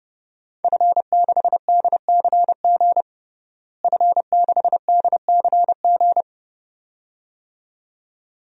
Morse code, heard as a keyed tone, about 30 wpm, 700 Hz. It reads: F6DCG F6DCG
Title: DXCC-100 Call Signs - 5 Characters - Repeated 2x - 30wpm